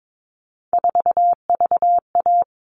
Morse code, heard as a keyed tone, about 22 wpm, 700 Hz. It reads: 4VA
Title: All Cut Numbers - 3-Digit 22wpm